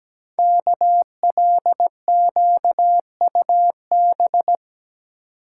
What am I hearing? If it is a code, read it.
Morse code, heard as a keyed tone, about 17 wpm, 700 Hz. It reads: KLQUB
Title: Code Groups - 5 Letter:Numbers - 17wpm